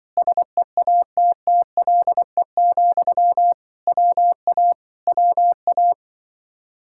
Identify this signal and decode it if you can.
Morse code, heard as a keyed tone, about 24 wpm, 700 Hz. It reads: SEATTLE, WA WA
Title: Speed-Racing Common QSO Phrases Encore - 20wpm